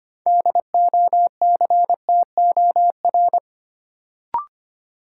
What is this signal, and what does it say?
Morse code, heard as a keyed tone, about 25 wpm, 700 Hz. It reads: DOCTOR
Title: Sentences from Top 1500 Words - New Word Drill Alt - 25wpm